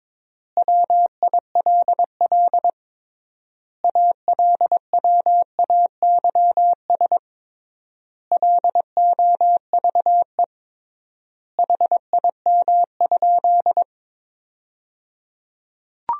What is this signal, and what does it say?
Morse code, heard as a keyed tone, about 21 wpm, 700 Hz. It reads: WILL ALWAYS LOVE HIM?
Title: 3x Word-Spacing - Sentences from Top 400 Words - 22wpm